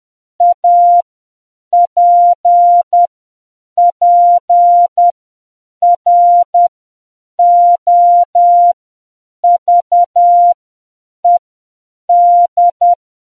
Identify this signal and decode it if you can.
Morse code, heard as a keyed tone, about 10 wpm, spaced as for 7 wpm, 700 Hz. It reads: APPROVED